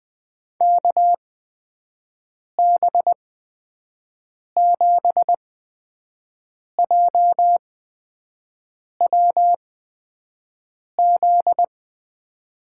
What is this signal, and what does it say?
Morse code, heard as a keyed tone, about 20 wpm, spaced as for 5 wpm, 700 Hz. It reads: KB7JWZ
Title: USA Call Signs - 6 Characters - 8x Character Spacing - 20wpm